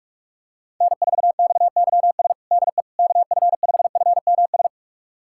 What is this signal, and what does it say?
Morse code, heard as a keyed tone, about 46 wpm, 700 Hz. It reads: N4XYS DE KF5FMS
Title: USA Callsign DE Callsign 45wpm